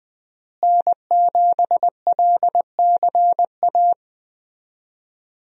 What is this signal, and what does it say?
Morse code, heard as a keyed tone, about 21 wpm, 700 Hz. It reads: N7LCA
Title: USA Call Signs - 5 Characters - Repeated 2x - 20wpm